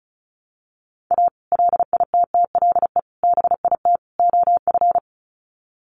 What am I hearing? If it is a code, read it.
Morse code, heard as a keyed tone, about 34 wpm, 700 Hz. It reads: A LITTLE BIT OF
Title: Sets of 4 Words 35wpm